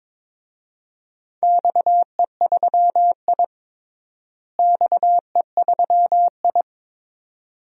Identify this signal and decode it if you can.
Morse code, heard as a keyed tone, about 22 wpm, 700 Hz. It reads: XE3I XE3I
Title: DXCC-100 Call Signs - 4 Characters - Repeated 2x - 22wpm